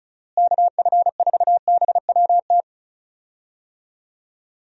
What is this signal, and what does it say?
Morse code, heard as a keyed tone, about 36 wpm, 700 Hz. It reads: KF4BWT
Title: USA Call Signs - 6 Characters - Repeated 3x - 35wpm